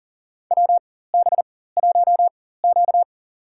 Morse code, heard as a keyed tone, about 40 wpm, spaced as for 19 wpm, 700 Hz. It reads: WB1Q
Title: USA Call Signs - 4 Characters - 4x Character Spacing - 40wpm